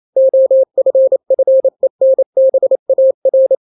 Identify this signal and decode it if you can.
Morse code, heard as a keyed tone, about 28 wpm, 530 Hz. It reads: OFFENBAR
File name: daily / 27wpm